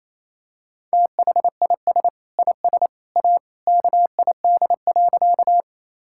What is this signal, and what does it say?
Morse code, heard as a keyed tone, about 28 wpm, 700 Hz. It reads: THIS IS A KID.
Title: Sentences from Top 1600 Words 28wpm